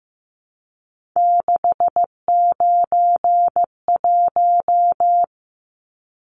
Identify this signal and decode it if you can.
Morse code, heard as a keyed tone, about 15 wpm, 700 Hz. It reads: 691
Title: Numbers - 3-Digit 15wpm